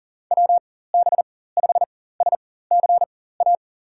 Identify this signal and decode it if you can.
Morse code, heard as a keyed tone, about 40 wpm, spaced as for 19 wpm, 700 Hz. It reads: WB5SCA
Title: USA Call Signs - 4x Character Spacing - 40wpm